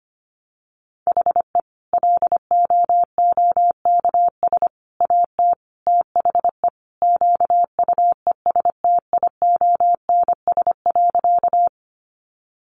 Morse code, heard as a keyed tone, about 25 wpm, 700 Hz. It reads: HE LOOKS AT THE QUESTIONS.
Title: Additional Easy Phrases 25wpm